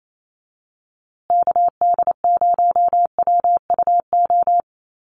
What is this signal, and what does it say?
Morse code, heard as a keyed tone, about 28 wpm, 700 Hz. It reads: KD0WUO
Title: USA Call Signs Repeated 3x 28wpm